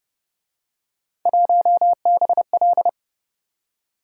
Morse code, heard as a keyed tone, about 33 wpm, 700 Hz. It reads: 1BL